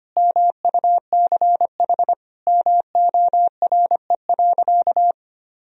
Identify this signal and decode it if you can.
Morse code, heard as a keyed tone, about 25 wpm, 700 Hz. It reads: MUCH MORE.
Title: Sentences from Top 1000 Words - Review Part 4 - 25wpm